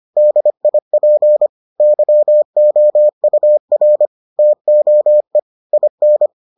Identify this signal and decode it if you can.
Morse code, heard as a keyed tone, about 25 wpm, 600 Hz. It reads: DIP YOUR TOE IN